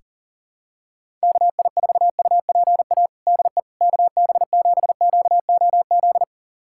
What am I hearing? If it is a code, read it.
Morse code, heard as a keyed tone, about 40 wpm, 700 Hz. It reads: KI4UPA DE KB7QOZ
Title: USA Callsign DE Callsign 40wpm